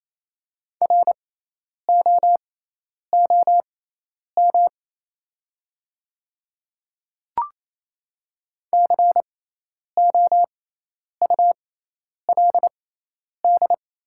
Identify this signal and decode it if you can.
Morse code, heard as a keyed tone, about 28 wpm, spaced as for 10 wpm, 690 Hz. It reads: ROOM COULD
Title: 6x Character Spacing - Top 300 Words - 28wpm